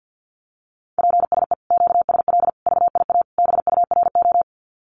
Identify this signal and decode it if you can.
Morse code, heard as a keyed tone, about 50 wpm, 700 Hz. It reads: PSE QSL VIA BURO
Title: Common QSO Phrases 50wpm